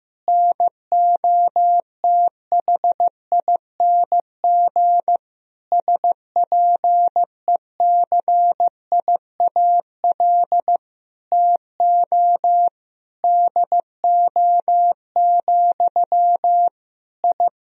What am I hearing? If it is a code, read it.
Morse code, heard as a keyed tone, about 15 wpm, 700 Hz. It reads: NOTHING SPECIAL TO DO, I
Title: Sentences from Top 1000 Words - Review Part 3 - 15wpm